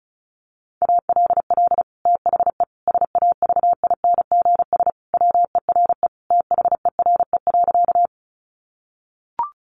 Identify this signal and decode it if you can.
Morse code, heard as a keyed tone, about 34 wpm, 700 Hz. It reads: ALL THE SAVINGS WERE THERE.
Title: Sentences from Top 1200 Words 35wpm